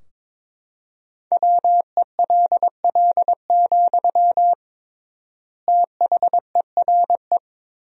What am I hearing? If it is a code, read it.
Morse code, heard as a keyed tone, about 22 wpm, 700 Hz. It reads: WELL, THERE
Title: 3x Word-Spacing - Sentences from Top 500 Words - 22wpm